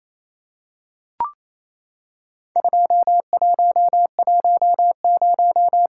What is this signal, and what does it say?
Morse code, heard as a keyed tone, about 28 wpm, 700 Hz. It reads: 2110
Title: Worked All Europe DX Contest - QTC Exchange - 28wpm